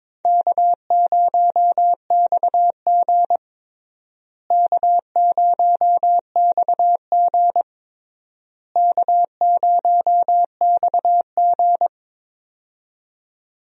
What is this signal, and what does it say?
Morse code, heard as a keyed tone, about 21 wpm, 700 Hz. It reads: K0XG K0XG K0XG
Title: USA Call Signs - 4 Characters - Repeated 3x - 22wpm